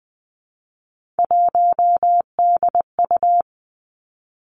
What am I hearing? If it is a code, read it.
Morse code, heard as a keyed tone, about 20 wpm, 700 Hz. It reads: 1DU
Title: Three Challenging Letter-Number Combinations 20wpm